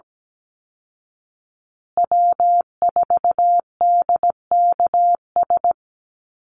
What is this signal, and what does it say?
Morse code, heard as a keyed tone, about 17 wpm, 700 Hz. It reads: W4DKS
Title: Call Signs Repeated 3x - 17wpm